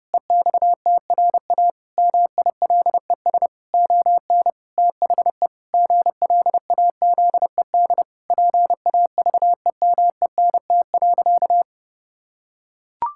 Morse code, heard as a keyed tone, about 30 wpm, 700 Hz. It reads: EXTRA MILES ON THE GLAZED PAVEMENT.
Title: Pangram Challenge v2 - 30wpm